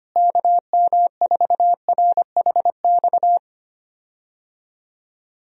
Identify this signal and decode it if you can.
Morse code, heard as a keyed tone, about 25 wpm, 700 Hz. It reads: KM4RHX
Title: USA Call Signs - 6 Characters - Repeated 3x - 25wpm